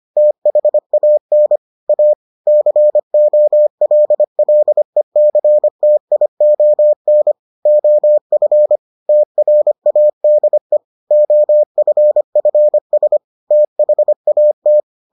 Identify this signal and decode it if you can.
Morse code, heard as a keyed tone, about 25 wpm, 600 Hz. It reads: THAN A COLLECTION OF TRADE OFFS THAT